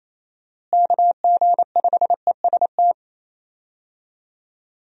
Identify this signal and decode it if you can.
Morse code, heard as a keyed tone, about 27 wpm, 700 Hz. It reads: KG5EST
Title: USA Call Signs - 6 Characters - Repeated 3x - 28wpm